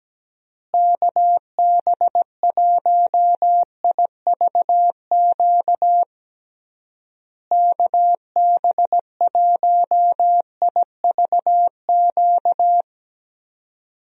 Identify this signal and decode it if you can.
Morse code, heard as a keyed tone, about 17 wpm, 700 Hz. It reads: KB1IVQ KB1IVQ
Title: USA Call Signs - 6 Characters - Repeated 3x - 17wpm